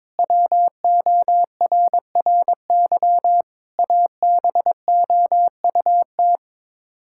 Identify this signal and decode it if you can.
Morse code, heard as a keyed tone, about 22 wpm, 700 Hz. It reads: WORRY ABOUT
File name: practice / 22wpm / Sets of 7 Words - 22wpm